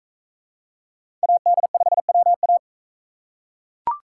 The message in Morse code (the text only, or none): AD5WA